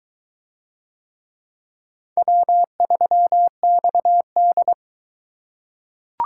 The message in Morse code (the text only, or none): W3XD